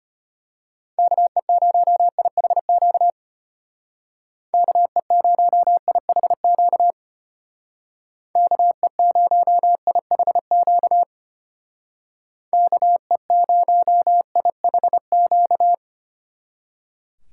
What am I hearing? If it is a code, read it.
Morse code, heard as a keyed tone, about 31 wpm, 700 Hz. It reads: KE0IHQ KE0IHQ KE0IHQ KE0IHQ
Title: Speed Racing - USA 2x3 Call Signs 25wpm